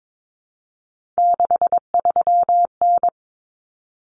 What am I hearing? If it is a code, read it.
Morse code, heard as a keyed tone, about 22 wpm, 700 Hz. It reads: 63N
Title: Common Cut Numbers - 3-Digit 22wpm